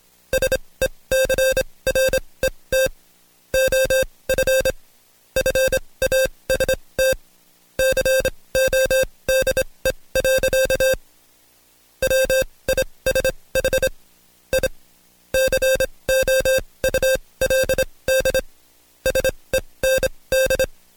Morse code, heard as a keyed tone, about 26 wpm, spaced as for 20 wpm, 530 Hz. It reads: SECRET OF FAST CODE. WISH I COULD SEND
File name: Code Quick / AudioDownld2 / Seg136-Stepper9 19WPM